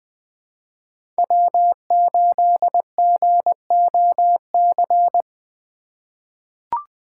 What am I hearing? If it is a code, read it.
Morse code, heard as a keyed tone, about 20 wpm, 700 Hz. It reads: W8GOC
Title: USA Call Signs - 5 Characters - 4x Character Spacing - 20wpm